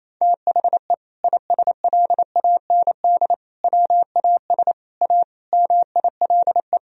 Morse code, heard as a keyed tone, about 28 wpm, 700 Hz. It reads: THE ISLAND WAS A MILE
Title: Sentences from Top 500 Words and Abbreviations 28wpm